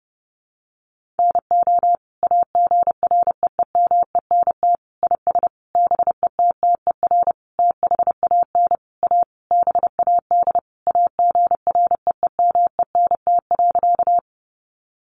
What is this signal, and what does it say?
Morse code, heard as a keyed tone, about 30 wpm, 700 Hz. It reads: NO AGREEMENT IS BETTER THAN A BAD AGREEMENT.
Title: Sentences from Top 1500 Words 30wpm